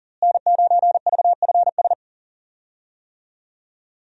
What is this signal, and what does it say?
Morse code, heard as a keyed tone, about 40 wpm, 700 Hz. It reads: N9VFS
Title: USA Call Signs - 5 Characters - Repeated 2x - 40wpm